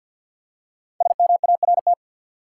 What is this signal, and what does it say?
Morse code, heard as a keyed tone, about 53 wpm, 700 Hz. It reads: SMART
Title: Five Letter Words - 50wpm